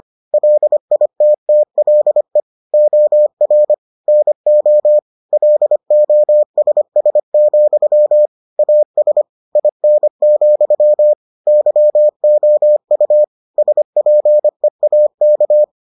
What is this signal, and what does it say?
Morse code, heard as a keyed tone, about 25 wpm, 600 Hz. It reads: LITTLE OR NO LOSS, AS IN, YOU SPEAK